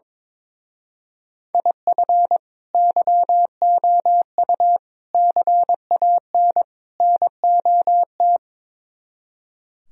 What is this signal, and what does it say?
Morse code, heard as a keyed tone, about 22 wpm, 700 Hz. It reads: IF YOU CAN NOT
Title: Speed-Racing - Sets of 4 Words 22wpm